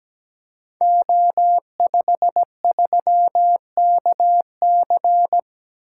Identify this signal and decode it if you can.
Morse code, heard as a keyed tone, about 17 wpm, 700 Hz. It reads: O53KC
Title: Code Groups - 5 Letter:Numbers - 17wpm